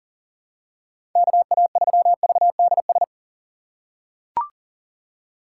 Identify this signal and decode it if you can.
Morse code, heard as a keyed tone, about 40 wpm, 700 Hz. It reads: KA3VDS
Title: USA Call Signs - 4x Character Spacing - 40wpm